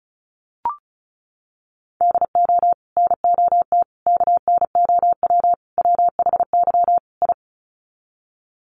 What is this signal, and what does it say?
Morse code, heard as a keyed tone, about 34 wpm, 700 Hz. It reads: DO NOT KNOW WHY I
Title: Sets of 5 Words Encore 35wpm